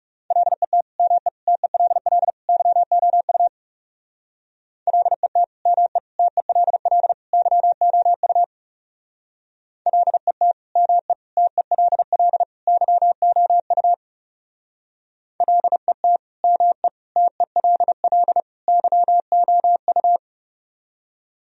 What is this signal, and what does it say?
Morse code, heard as a keyed tone, about 36 wpm, 700 Hz. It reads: LET ME TELL YOU LET ME TELL YOU LET ME TELL YOU LET ME TELL YOU
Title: Speed-Racing - Sets of 4 Words 30wpm